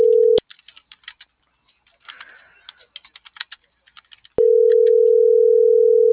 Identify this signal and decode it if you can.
Ringback tone (North America), 440 + 480 Hz.
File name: CENTRAL MAERSK / out isabel.bravo - .90112